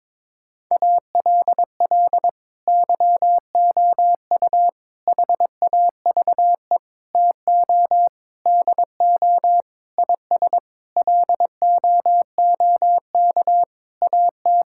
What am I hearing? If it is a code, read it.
Morse code, heard as a keyed tone, about 22 wpm, 700 Hz. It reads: ALL YOU HAVE TO DO IS LOOK AT
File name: Sets of 9 Words - 22wpm